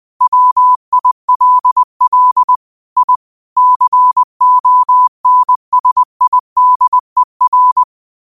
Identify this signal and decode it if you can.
Morse code, heard as a keyed tone, about 20 wpm, 1000 Hz. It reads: WILL I CONSIDER
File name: qotdfiles / 20wpm